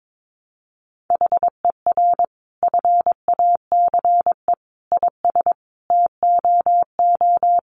Morse code, heard as a keyed tone, about 22 wpm, 700 Hz. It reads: HER FACE IS TOO
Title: Sentences from Top 1200 Words 22wpm